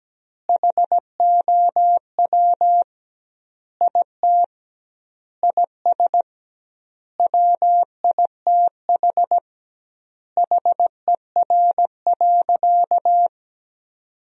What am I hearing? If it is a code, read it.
Morse code, heard as a keyed tone, about 17 wpm, 700 Hz. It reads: HOW IT IS WITH HER.
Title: 2x Word-Spacing - Sentences from Top 100 Words - 17wpm